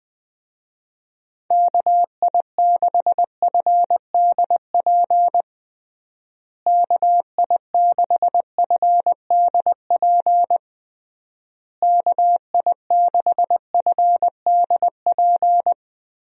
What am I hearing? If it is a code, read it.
Morse code, heard as a keyed tone, about 20 wpm, 700 Hz. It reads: KI6FDP KI6FDP KI6FDP
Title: USA Call Signs - 6 Characters - Repeated 3x - 20wpm